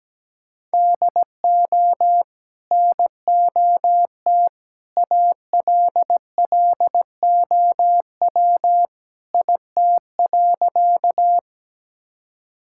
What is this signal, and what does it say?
Morse code, heard as a keyed tone, about 17 wpm, 700 Hz. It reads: DO NOT ALLOW IT.
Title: Sentences from Top 1000 Words - Review Part 5 - 17wpm